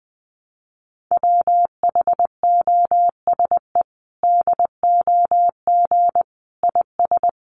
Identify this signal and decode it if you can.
Morse code, heard as a keyed tone, about 20 wpm, 700 Hz. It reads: WHOSE DOG IS
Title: Sentences from Top 1200 Words 20wpm